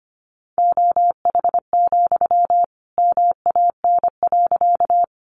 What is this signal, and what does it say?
Morse code, heard as a keyed tone, about 25 wpm, 700 Hz. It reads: OH, MAN.
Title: Sentences from Top 500 Words 25wpm